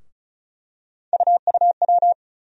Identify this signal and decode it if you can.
Morse code, heard as a keyed tone, about 36 wpm, 700 Hz. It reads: UUW
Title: USA Call Sign Postfixes - 3 Letters 35wpm